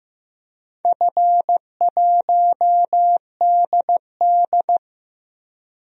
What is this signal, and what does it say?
Morse code, heard as a keyed tone, about 15 wpm, 700 Hz. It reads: F1DD
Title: DXCC-100 Call Signs - 4 Characters - 15wpm